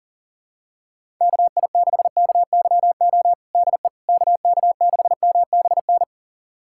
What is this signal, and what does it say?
Morse code, heard as a keyed tone, about 40 wpm, 700 Hz. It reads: KI6KYO DE KK6MBN
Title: USA Callsign DE Callsign - 6 Characters - 40wpm